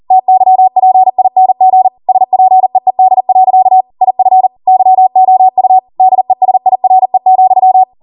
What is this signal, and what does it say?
Morse code, heard as a keyed tone, about 40 wpm, 750 Hz. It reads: TYPING SPEED. IF YOU DESIRE,